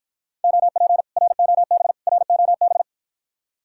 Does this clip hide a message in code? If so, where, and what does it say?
Yes: Morse code from 0.4 to 2.8 s, heard as a keyed tone, about 53 wpm, 700 Hz. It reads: OP ROB ROB